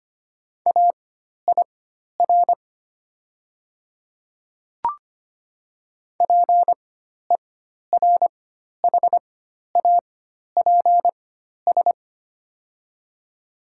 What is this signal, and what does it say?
Morse code, heard as a keyed tone, about 25 wpm, spaced as for 12 wpm, 700 Hz. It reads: AIR PERHAPS
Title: 4x Character Spacing - Top 500 Words - 25wpm